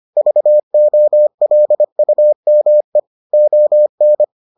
Morse code, heard as a keyed tone, about 25 wpm, 600 Hz. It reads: VOLUME ON